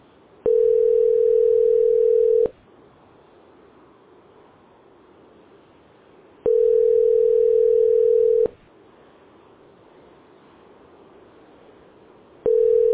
Ringback tone (North America), 440 + 480 Hz.